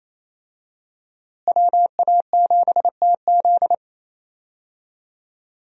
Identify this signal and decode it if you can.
Morse code, heard as a keyed tone, about 28 wpm, 700 Hz. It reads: WA7TZ
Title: USA Call Signs - 5 Characters - Repeated 2x - 28wpm